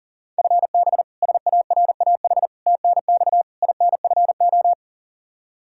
Morse code, heard as a keyed tone, about 40 wpm, 700 Hz. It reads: FB SARAH TNX INFO
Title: K1USN SST QSO Practice 40wpm